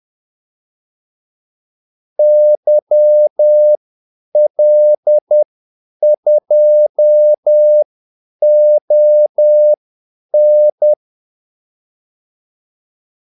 Morse code, heard as a keyed tone, about 10 wpm, spaced as for 8 wpm, 600 Hz. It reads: YL2ON
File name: Callsigns-10-8wpm